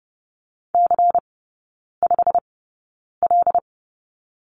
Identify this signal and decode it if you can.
Morse code, heard as a keyed tone, about 30 wpm, spaced as for 9 wpm, 690 Hz. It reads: C5L
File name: Lesson 16e - Letter c - Three Characters 30wpm